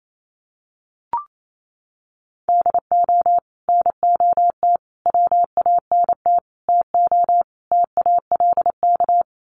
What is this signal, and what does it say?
Morse code, heard as a keyed tone, about 28 wpm, 700 Hz. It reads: DO NOT WANT TO TALK
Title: Sets of 5 Words 28wpm